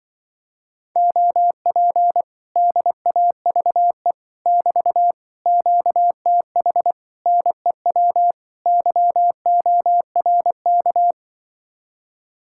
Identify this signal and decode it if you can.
Morse code, heard as a keyed tone, about 25 wpm, 700 Hz. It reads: OP DAVE <BT> QTH NEW YORK